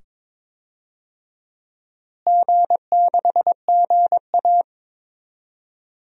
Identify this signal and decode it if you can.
Morse code, heard as a keyed tone, about 22 wpm, 700 Hz. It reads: G6GA
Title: DXCC-100 Call Signs - 4 Characters - Repeated 2x - 22wpm